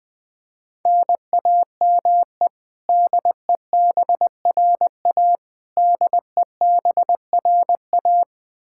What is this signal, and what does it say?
Morse code, heard as a keyed tone, about 21 wpm, 700 Hz. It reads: NAME DEBRA DEBRA
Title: Practice QSO - Exchange 1 - Beginning - 20wpm